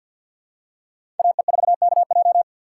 Morse code, heard as a keyed tone, about 47 wpm, 690 Hz. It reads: AE4KJ
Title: USA Call Signs - 5 Characters - 6x Character Spacing - 50wpm